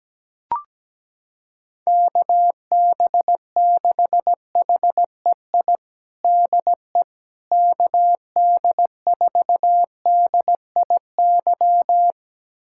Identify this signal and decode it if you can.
Morse code, heard as a keyed tone, about 17 wpm, 700 Hz. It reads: KB6HEI DE KD4DIY
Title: USA Callsign DE Callsign - 6 Characters - 17wpm